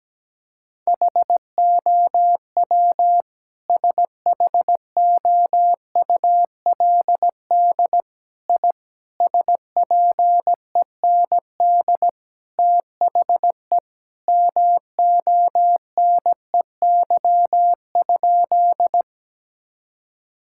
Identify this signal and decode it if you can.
Morse code, heard as a keyed tone, about 17 wpm, 700 Hz. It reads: HOW SHOULD I SPEND THE MONEY?
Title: Sentences from Top 1000 Words - Review Part 5 - 17wpm